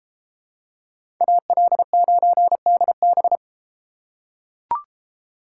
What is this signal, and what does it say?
Morse code, heard as a keyed tone, about 30 wpm, 700 Hz. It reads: AL9DB